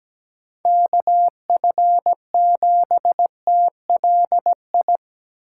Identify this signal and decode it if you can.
Morse code, heard as a keyed tone, about 20 wpm, 700 Hz. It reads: KF7TLI
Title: Speed Racing - USA 2x3 Call Signs 17wpm